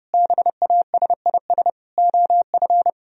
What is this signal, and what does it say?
Morse code, heard as a keyed tone, about 30 wpm, 700 Hz. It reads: BASIS OF